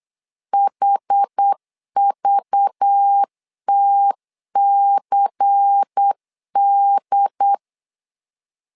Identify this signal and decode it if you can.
Morse code, heard as a keyed tone, about 8 wpm, 800 Hz. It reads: HVTCD
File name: LEZIONI / 5 LBVC / 5 LBVC 8wpm 2935